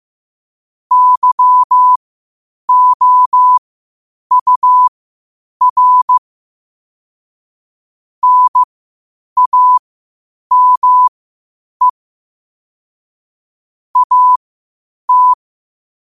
Morse code, heard as a keyed tone, about 15 wpm, spaced as for 8 wpm, 1000 Hz. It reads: YOUR NAME AT
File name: qotdfiles / 8wpm